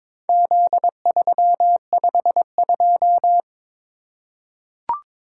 Morse code, heard as a keyed tone, about 22 wpm, 700 Hz. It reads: Z352
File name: practice / 22wpm / Four Letter-Number Combinations - 22wpm